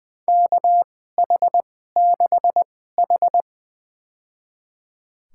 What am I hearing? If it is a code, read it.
Morse code, heard as a keyed tone, about 21 wpm, spaced as for 15 wpm, 700 Hz. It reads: KH6H
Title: USA Call Signs - 2x Character Spacing - 20wpm